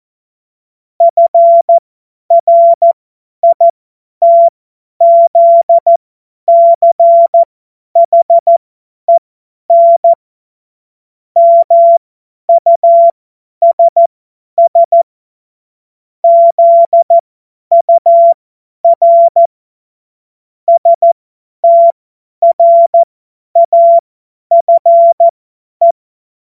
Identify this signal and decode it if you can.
Morse code, heard as a keyed tone, about 14 wpm, spaced as for 10 wpm, 680 Hz. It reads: FRITZCHEN MUSS ZUR STRAFE
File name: Morsekurs / Witze / Witz 13. September 2022 50 F70